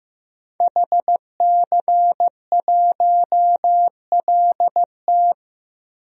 Morse code, heard as a keyed tone, about 15 wpm, 700 Hz. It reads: HC1LT
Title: DXCC-100 Call Signs - 5 Characters - 15wpm